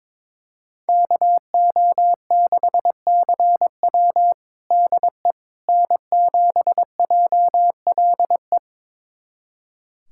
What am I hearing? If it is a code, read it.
Morse code, heard as a keyed tone, about 21 wpm, 700 Hz. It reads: KO6CW DE N7JLE
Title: USA Callsign DE Callsign - 5 Characters - 22wpm